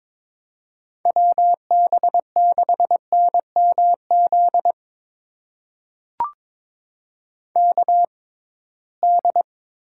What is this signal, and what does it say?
Morse code, heard as a keyed tone, about 22 wpm, spaced as for 17 wpm, 700 Hz. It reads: WB6NMZ KD
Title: USA Call Signs - 6 Characters - 6x Character Spacing - 22wpm